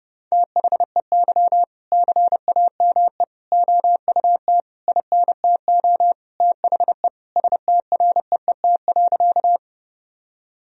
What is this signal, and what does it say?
Morse code, heard as a keyed tone, about 30 wpm, 700 Hz. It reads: THEY CAME OUT INTO THE STREET.
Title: Sentences from Top 1000 Words - Review Part 3 - 30wpm